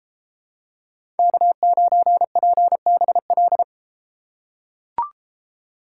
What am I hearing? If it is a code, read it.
Morse code, heard as a keyed tone, about 33 wpm, 700 Hz. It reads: K9PBL